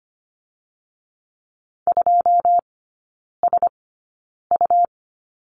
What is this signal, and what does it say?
Morse code, heard as a keyed tone, about 25 wpm, spaced as for 9 wpm, 700 Hz. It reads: 2SU
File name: Lesson 17e - Letter u - Three Characters 25wpm